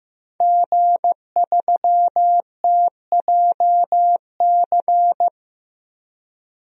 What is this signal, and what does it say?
Morse code, heard as a keyed tone, about 15 wpm, 700 Hz. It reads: G3TJC
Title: DXCC-100 Call Signs - 5 Characters - 15wpm